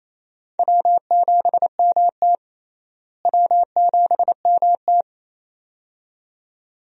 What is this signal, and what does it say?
Morse code, heard as a keyed tone, about 28 wpm, 700 Hz. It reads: W7MT W7MT
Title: USA Call Signs - 4 Characters - Repeated 2x - 28wpm